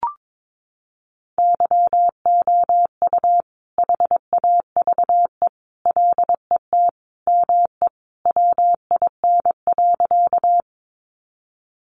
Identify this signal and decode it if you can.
Morse code, heard as a keyed tone, about 22 wpm, 700 Hz. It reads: YOU HAVE LET ME WIN.
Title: Sentences from Top 1500 Words 22wpm